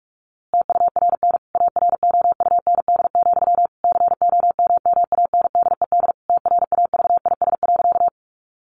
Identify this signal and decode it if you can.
Morse code, heard as a keyed tone, about 44 wpm, 700 Hz. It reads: TURN AROUND, COMMANDED TRAVIS.
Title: A Sound Of Thunder 45wpm